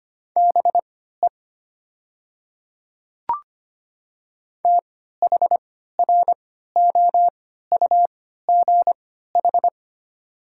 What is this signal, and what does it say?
Morse code, heard as a keyed tone, about 25 wpm, spaced as for 14 wpm, 700 Hz. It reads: BE THROUGH
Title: 3x Character Spacing - Top 200 Words - 25wpm